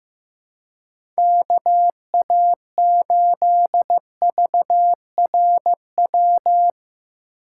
Morse code, heard as a keyed tone, about 15 wpm, 700 Hz. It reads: KA8VRW
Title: USA Call Signs - 6 Characters - 4x Character Spacing - 15wpm